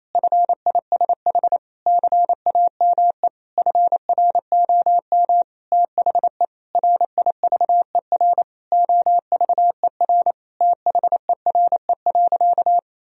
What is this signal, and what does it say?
Morse code, heard as a keyed tone, about 28 wpm, 700 Hz. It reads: FISH CAME FROM THE RIVER OVER THERE.
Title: Sentences from Top 1000 Words - Review Part 2 - 28wpm